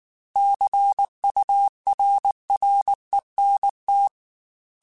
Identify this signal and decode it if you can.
Morse code, heard as a keyed tone, about 19 wpm, 800 Hz. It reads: CURRENT